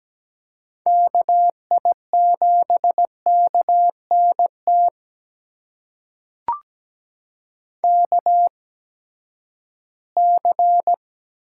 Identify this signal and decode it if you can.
Morse code, heard as a keyed tone, about 17 wpm, spaced as for 12 wpm, 700 Hz. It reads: KI7KNT KC
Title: USA Call Signs - 6 Characters - 8x Character Spacing - 17wpm